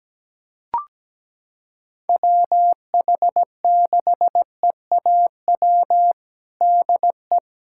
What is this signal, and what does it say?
Morse code, heard as a keyed tone, about 17 wpm, 700 Hz. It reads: WH6EAW DE
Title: USA Callsign DE Callsign 17wpm